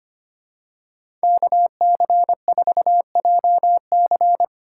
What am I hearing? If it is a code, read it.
Morse code, heard as a keyed tone, about 25 wpm, 700 Hz. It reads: KC4JC
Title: USA Call Signs - 5 Characters - 8x Character Spacing - 25wpm